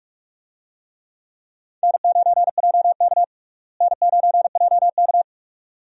Morse code, heard as a keyed tone, about 45 wpm, 700 Hz. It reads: N9JK N9JK
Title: USA Call Signs - 4 Characters - Repeated 2x - 45wpm